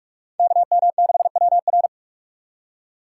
Morse code, heard as a keyed tone, about 46 wpm, 700 Hz. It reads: KM6WR
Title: USA Call Signs - 5 Characters - 2x Character Spacing - 45wpm